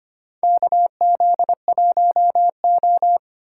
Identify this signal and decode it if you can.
Morse code, heard as a keyed tone, about 25 wpm, 700 Hz. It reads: KZ1O
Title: USA Call Signs - 4 Characters - 3x Character Spacing - 25wpm